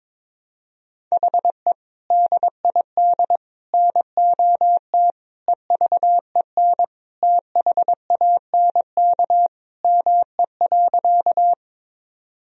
Morse code, heard as a keyed tone, about 22 wpm, 700 Hz. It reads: HE DID NOT EVEN THANK ME.
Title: Sentences from Top 1000 Words - Review Part 1 - 22wpm